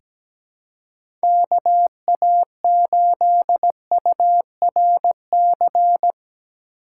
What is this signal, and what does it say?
Morse code, heard as a keyed tone, about 17 wpm, 700 Hz. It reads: KA8URC
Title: USA Call Signs - 6 Characters - 6x Character Spacing - 17wpm